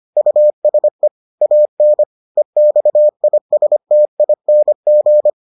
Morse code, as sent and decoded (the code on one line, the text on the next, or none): ..- ... / .- -. / . -..- .. ... - .. -. --.
USE AN EXISTING